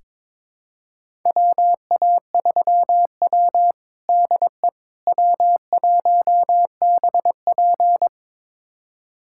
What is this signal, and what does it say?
Morse code, heard as a keyed tone, about 22 wpm, 700 Hz. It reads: WA3W DE W1BP
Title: USA Callsign DE Callsign - 4 Characters - 22wpm